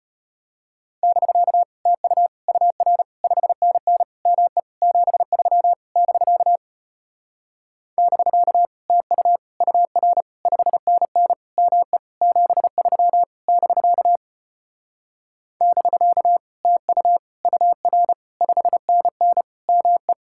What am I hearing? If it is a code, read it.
Morse code, heard as a keyed tone, about 33 wpm, 700 Hz. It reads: <BK> TU UR 5NN ME 73 <BK> <BK> TU UR 5NN ME 73 <BK> <BK> TU UR 5NN ME